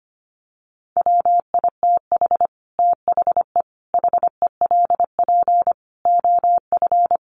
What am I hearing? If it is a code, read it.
Morse code, heard as a keyed tone, about 25 wpm, 700 Hz. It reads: WITH THE HELP OF
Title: Sets of 4 Words 25wpm